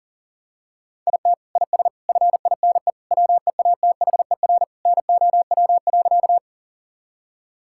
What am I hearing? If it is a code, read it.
Morse code, heard as a keyed tone, about 40 wpm, 700 Hz. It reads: IT IS FINE WEATHER NOW.
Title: Sentences from Top 1000 Words - Review Part 5 - 40wpm